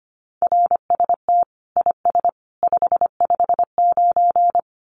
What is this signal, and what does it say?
Morse code, heard as a keyed tone, about 25 wpm, 700 Hz. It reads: RST IS 559
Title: Common QSO Phrases 25wpm